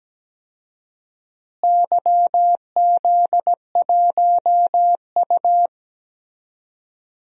Morse code, heard as a keyed tone, about 17 wpm, 700 Hz. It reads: YZ1U
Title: DXCC-100 Call Signs - 4 Characters - Repeated 3x - 17wpm